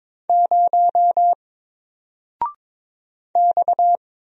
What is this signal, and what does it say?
Morse code, heard as a keyed tone, about 22 wpm, 700 Hz. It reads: 0 X
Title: Single Character (Letter:Number:Punctuation:Prosign) 22wpm